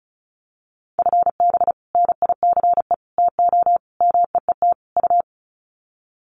Morse code, heard as a keyed tone, about 34 wpm, 700 Hz. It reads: FB NICE TO MEET U
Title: Common QSO Phrases 35wpm